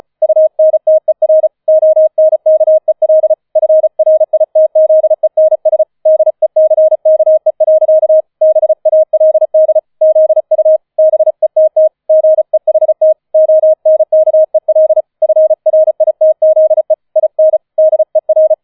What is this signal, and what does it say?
Morse code, heard as a keyed tone, about 35 wpm, 630 Hz. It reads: UNTER ONKEL FRITZENS DECKE. BALD ZU BETT GEHT ONKEL FRITZE IN DER
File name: 005 Fünfter Streich 35wpm